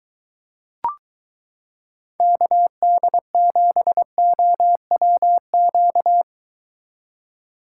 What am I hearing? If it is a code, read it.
Morse code, heard as a keyed tone, about 23 wpm, 700 Hz. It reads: KD7OWQ